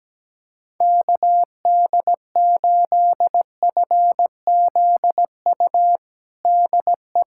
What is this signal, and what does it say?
Morse code, heard as a keyed tone, about 17 wpm, 700 Hz. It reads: KD8FZU DE
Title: USA Callsign DE Callsign 17wpm